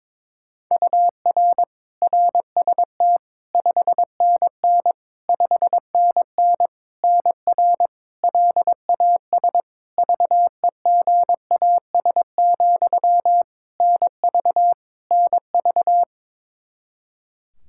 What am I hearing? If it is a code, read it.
Morse code, heard as a keyed tone, about 22 wpm, 700 Hz. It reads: UR RST 5NN 5NN NR LAS VEGAS, NV NV
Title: K1USN SST QSO Practice 22wpm